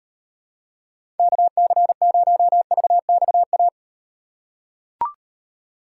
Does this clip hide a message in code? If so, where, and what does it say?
Yes: Morse code from 1.2 to 3.7 s, heard as a keyed tone, about 38 wpm, 700 Hz. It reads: KC0VXA